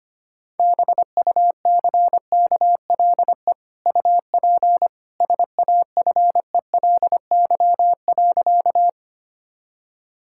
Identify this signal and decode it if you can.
Morse code, heard as a keyed tone, about 25 wpm, 700 Hz. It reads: BUCKLE UP SAFELY.
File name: Easy Sentences with Hints - 25wpm